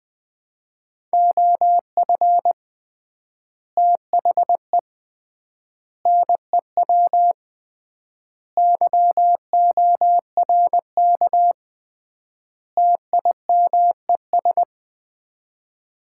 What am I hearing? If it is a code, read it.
Morse code, heard as a keyed tone, about 21 wpm, 700 Hz. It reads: OF THE NEW YORK TIMES
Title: 3x Word-Spacing - Sets of 5 Words - 20wpm